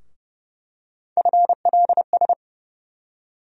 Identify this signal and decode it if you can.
Morse code, heard as a keyed tone, about 30 wpm, 700 Hz. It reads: FLS
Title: USA Call Sign Postfixes - 3 Letters 30wpm